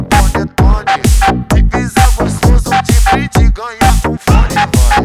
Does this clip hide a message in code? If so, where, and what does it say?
No.